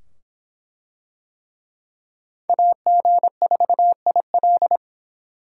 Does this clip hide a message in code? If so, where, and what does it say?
Yes: Morse code from 2.5 to 4.8 s, heard as a keyed tone, about 23 wpm, 700 Hz. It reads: AG4IL